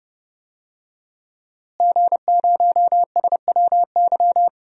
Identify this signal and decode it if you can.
Morse code, heard as a keyed tone, about 30 wpm, 700 Hz. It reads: G0SWY
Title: DXCC-100 Call Signs - 5 Characters - Repeated 2x - 30wpm